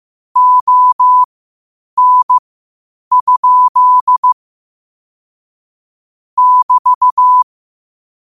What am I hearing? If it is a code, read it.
Morse code, heard as a keyed tone, about 15 wpm, spaced as for 8 wpm, 1000 Hz. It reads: ON? <BT>
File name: qotdfiles / 8wpm